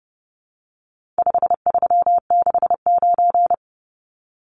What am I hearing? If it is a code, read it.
Morse code, heard as a keyed tone, about 30 wpm, 700 Hz. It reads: 5369
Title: Numbers - 4-Digit 30wpm